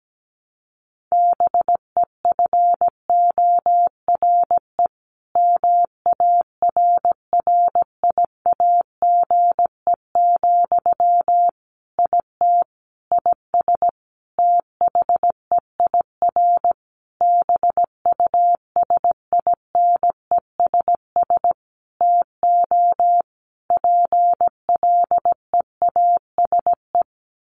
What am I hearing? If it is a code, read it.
Morse code, heard as a keyed tone, about 17 wpm, 700 Hz. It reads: BEFORE MARRIAGE, IT IS THEIR BUSINESS TO PLEASE
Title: Sentences from Top 1500 Words 17wpm